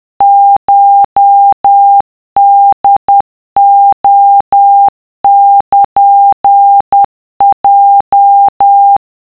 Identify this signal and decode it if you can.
Morse code (Japanese Wabun code), heard as a keyed tone, about 10 wpm, 800 Hz. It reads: コホレルヲ